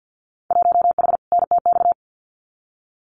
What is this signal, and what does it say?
Morse code, heard as a keyed tone, about 50 wpm, 700 Hz. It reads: 1H NTX
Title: Winter Field Day Exchanges 50wpm